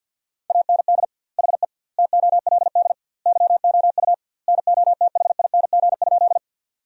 Morse code, heard as a keyed tone, about 50 wpm, 700 Hz. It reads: AND HE TOLD YOU NOTHING?
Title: 2x Word-Spacing - Sentences from Top 500 Words - 50wpm